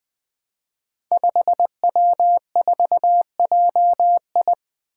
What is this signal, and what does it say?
Morse code, heard as a keyed tone, about 20 wpm, 700 Hz. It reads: 5W4JI
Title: Five Challenging Letter-Number Combinations 20wpm